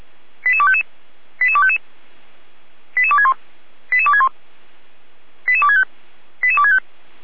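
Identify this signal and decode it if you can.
Five-tone selective call (ZVEI), heard as quick runs of five tones, tones of about 80 ms. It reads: 88255 88255 88261 88261 88265 88265
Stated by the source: ZVEI five-tone call 88231 received by SDR.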